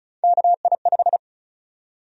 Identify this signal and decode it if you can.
Morse code, heard as a keyed tone, about 34 wpm, 700 Hz. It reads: KI5
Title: USA Call Sign Prefixes - 35wpm